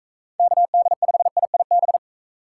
Morse code, heard as a keyed tone, about 42 wpm, 700 Hz. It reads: KD5IIB